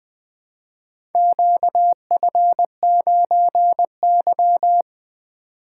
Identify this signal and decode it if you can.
Morse code, heard as a keyed tone, about 20 wpm, 700 Hz. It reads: QF9Y
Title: Four Letter-Number Combinations - 20wpm